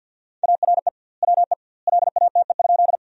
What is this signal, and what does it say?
Morse code, heard as a keyed tone, about 50 wpm, 700 Hz. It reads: ARE WE LATE?
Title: 2x Word-Spacing - Sentences from Top 300 Words - 50wpm